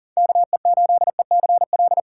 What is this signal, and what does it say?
Morse code, heard as a keyed tone, about 52 wpm, 700 Hz. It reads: KE8ECL